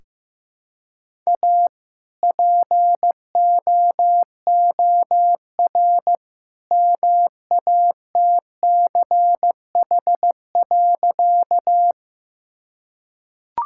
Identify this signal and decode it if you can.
Morse code, heard as a keyed tone, about 15 wpm, 700 Hz. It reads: A POOR MATCH.
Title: Sentences from Top 1600 Words 15wpm